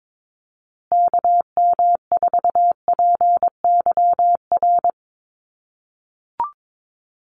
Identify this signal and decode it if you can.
Morse code, heard as a keyed tone, about 22 wpm, 700 Hz. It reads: KM4PYR
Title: USA Call Signs Repeated 2x 22wpm